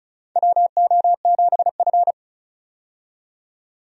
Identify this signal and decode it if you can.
Morse code, heard as a keyed tone, about 34 wpm, 700 Hz. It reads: WO7F
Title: USA Call Signs - 4 Characters - 35wpm